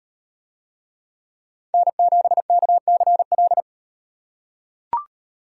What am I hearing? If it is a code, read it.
Morse code, heard as a keyed tone, about 38 wpm, 700 Hz. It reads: N7KCL